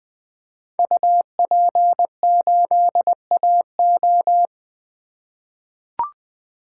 Morse code, heard as a keyed tone, about 20 wpm, 700 Hz. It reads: UP8AO
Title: Code Groups - 5 Letter:Numbers - 20wpm